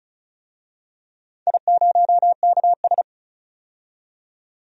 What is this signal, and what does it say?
Morse code, heard as a keyed tone, about 34 wpm, 700 Hz. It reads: I0KS
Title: DXCC-100 Call Signs - 4 Characters - 35wpm